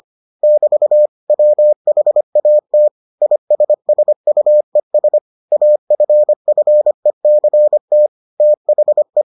Morse code, heard as a keyed tone, about 25 wpm, 600 Hz. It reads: <BT> WHAT ISSUES AFFECT THE